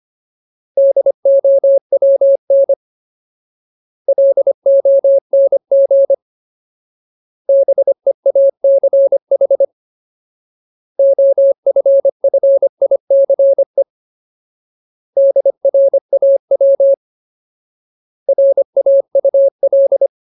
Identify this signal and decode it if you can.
Morse code, heard as a keyed tone, about 25 wpm, 550 Hz. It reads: DOWN LONG BEACH OFFICE DRAW RAUL
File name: Ultimate Word List / 25WPM Ultimate